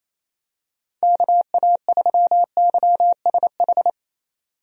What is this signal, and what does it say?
Morse code, heard as a keyed tone, about 28 wpm, 700 Hz. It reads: KA3YSH
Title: USA Call Signs - 6 Characters - 28wpm